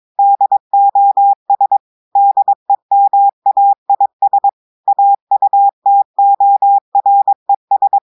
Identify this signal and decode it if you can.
Morse code, heard as a keyed tone, about 22 wpm, 800 Hz. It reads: DOS DEMAIS AUTORES
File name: audios / portugais / portugais texte en clair 1 vitesse 22 mots minutes